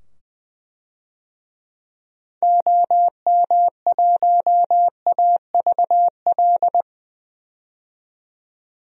Morse code, heard as a keyed tone, about 21 wpm, 700 Hz. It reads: OM1AVL
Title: DXCC-100 Call Signs - 6 Characters - 20wpm